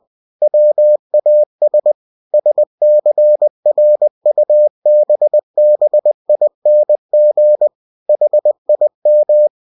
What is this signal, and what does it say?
Morse code, heard as a keyed tone, about 20 wpm, 600 Hz. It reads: WAS SCRUBBING HIM
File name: cw-books / winnie-the-pooh / winnie-20wpm-page-0072